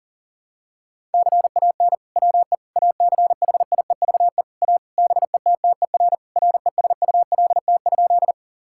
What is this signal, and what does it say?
Morse code, heard as a keyed tone, about 40 wpm, 700 Hz. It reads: CAN WE ACHIEVE A BETTER RESULT?
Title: Sentences from Top 1000 Words - Review Part 6 - 40wpm